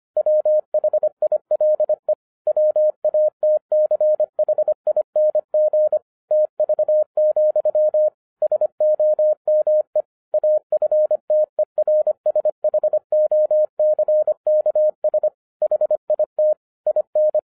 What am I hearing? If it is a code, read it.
Morse code, heard as a keyed tone, about 25 wpm, 610 Hz. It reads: WHILE WATCHING TV, SOME AFTERSHOCKS HIT IN
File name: 2011-07-mp3 / n34 THE 3.11 2 25wpm